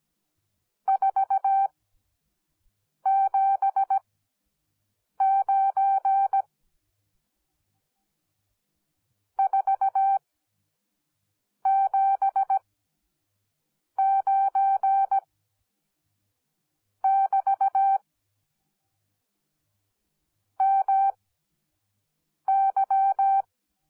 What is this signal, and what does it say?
Morse code, heard as a keyed tone, about 17 wpm, spaced as for 6 wpm, 780 Hz. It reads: 479 479 <BT> MY
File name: Onlinelehrgang / morsen / DJ4UF Morsen10